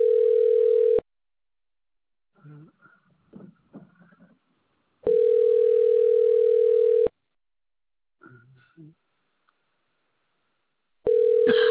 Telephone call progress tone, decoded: ringback tone